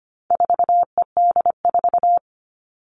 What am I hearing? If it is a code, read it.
Morse code, heard as a keyed tone, about 25 wpm, 700 Hz. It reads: 4ED4
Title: All Cut Numbers - 4-Digit 25wpm